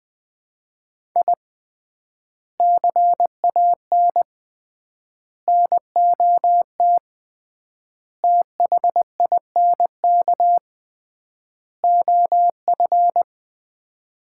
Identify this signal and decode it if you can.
Morse code, heard as a keyed tone, about 20 wpm, 700 Hz. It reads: I CAN NOT THINK OF
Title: 3x Word-Spacing - Sets of 5 Words - 20wpm